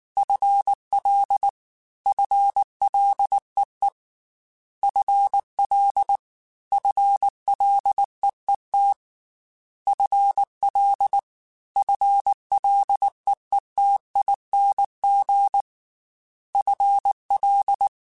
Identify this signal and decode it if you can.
Morse code, heard as a keyed tone, about 19 wpm, 800 Hz. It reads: FL FLEE FL FLEET FL FLEETING FL
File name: FL / FL mp3s / FL Prefixes 1x1 @18